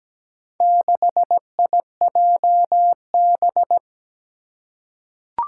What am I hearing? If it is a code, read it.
Morse code, heard as a keyed tone, about 17 wpm, 700 Hz. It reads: 6IJB
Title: Four Challenging Letter-Number Combinations 17wpm